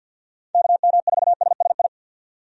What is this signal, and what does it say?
Morse code, heard as a keyed tone, about 52 wpm, 700 Hz. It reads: KM4SSI
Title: USA Call Signs - 6 Characters - 50wpm